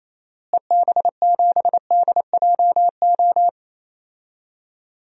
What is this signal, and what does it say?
Morse code, heard as a keyed tone, about 28 wpm, 700 Hz. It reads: EB7DJO
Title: DXCC-100 Call Signs - 6 Characters - Repeated 2x - 28wpm